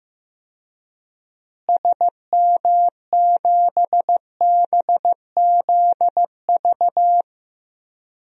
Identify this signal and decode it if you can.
Morse code, heard as a keyed tone, about 15 wpm, 700 Hz. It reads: SM7BZV
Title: DXCC-100 Call Signs - 6 Characters - Repeated 3x - 15wpm